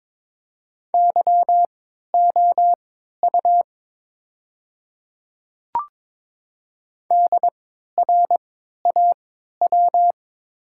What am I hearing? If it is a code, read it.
Morse code, heard as a keyed tone, about 22 wpm, spaced as for 13 wpm, 700 Hz. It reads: YOU DRAW
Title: 3x Character Spacing - Top 300 Words - 22wpm